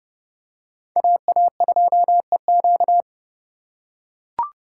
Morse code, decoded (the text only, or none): AA2EQ